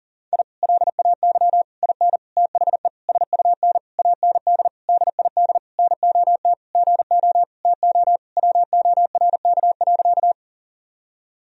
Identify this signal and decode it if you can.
Morse code, heard as a keyed tone, about 40 wpm, 700 Hz. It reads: I LAY IN THE SUN AND DID NOT GO TO WORK.
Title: Sentences from Top 1000 Words - Review Part 2 - 40wpm